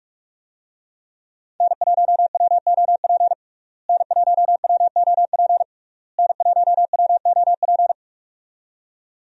Morse code, heard as a keyed tone, about 47 wpm, 700 Hz. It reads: N1WOP N1WOP N1WOP
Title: USA Call Signs - 5 Characters - Repeated 3x - 45wpm